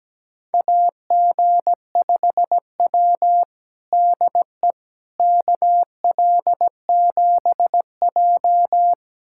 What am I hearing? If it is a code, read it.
Morse code, heard as a keyed tone, about 17 wpm, 700 Hz. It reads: AG5W DE KL7J
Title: USA Callsign DE Callsign - 4 Characters - 17wpm